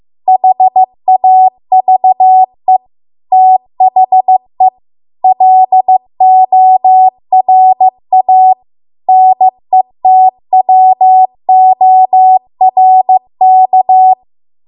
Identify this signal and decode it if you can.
Morse code, heard as a keyed tone, about 15 wpm, 750 Hz. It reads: HAVE THE LORA NETWORK